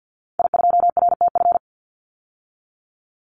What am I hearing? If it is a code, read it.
Morse code, heard as a keyed tone, about 50 wpm, 700 Hz. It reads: I2RTF
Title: Call Signs - 50wpm